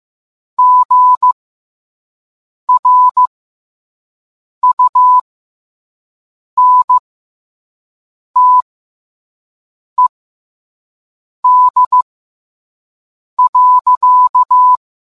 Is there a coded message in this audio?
Yes: Morse code from 0.6 to 14.8 s, heard as a keyed tone, about 15 wpm, spaced as for 5 wpm, 1000 Hz. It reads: GRUNTED.